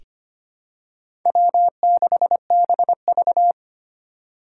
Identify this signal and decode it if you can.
Morse code, heard as a keyed tone, about 25 wpm, 700 Hz. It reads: W6BV
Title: Four Challenging Letter-Number Combinations 25wpm